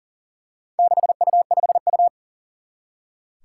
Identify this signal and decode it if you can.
Morse code, heard as a keyed tone, about 38 wpm, 700 Hz. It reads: 6U5U
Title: Four Challenging Letter-Number Combinations 40wpm